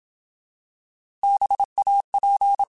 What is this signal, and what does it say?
Morse code, heard as a keyed tone, about 27 wpm, 800 Hz. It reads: BAP
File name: Alphabetical / Alphabetical mp3s / Bs @25